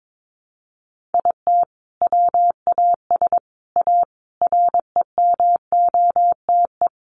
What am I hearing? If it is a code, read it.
Morse code, heard as a keyed tone, about 22 wpm, 700 Hz. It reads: IT WAS A REMOTE
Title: Sentences from Top 1300 Words 22wpm